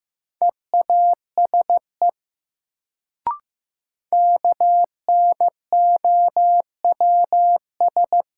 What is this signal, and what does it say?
Morse code, heard as a keyed tone, about 15 wpm, 700 Hz. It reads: EASE KNOWS
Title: Sentences from Top 1000 Words - Review Part 1 - New Word Drill - 15wpm